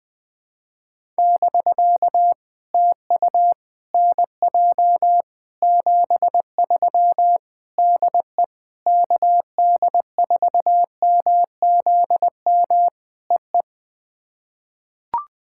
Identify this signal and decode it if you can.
Morse code, heard as a keyed tone, about 20 wpm, 700 Hz. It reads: <BK> TU NJ 73 DE KD4MZM EE
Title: POTA QSO Practice - 20wpm